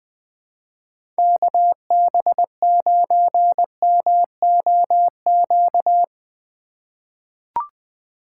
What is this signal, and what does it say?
Morse code, heard as a keyed tone, about 20 wpm, 700 Hz. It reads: KB9MOQ
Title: USA Call Signs - 6 Characters - 2x Character Spacing - 20wpm